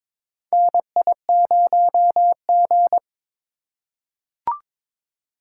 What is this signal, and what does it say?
Morse code, heard as a keyed tone, about 22 wpm, 700 Hz. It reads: NI0G
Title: USA Call Signs - 4 Characters - 4x Character Spacing - 22wpm